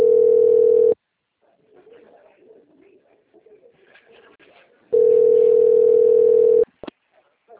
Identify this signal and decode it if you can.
Ringback tone (North America), 440 + 480 Hz.
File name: out- - - - - .10005